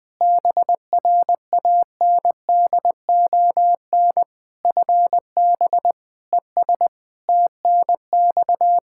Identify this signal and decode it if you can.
Morse code, heard as a keyed tone, about 20 wpm, 700 Hz. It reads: BRANDON FB ES TNX
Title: K1USN SST QSO Practice 20wpm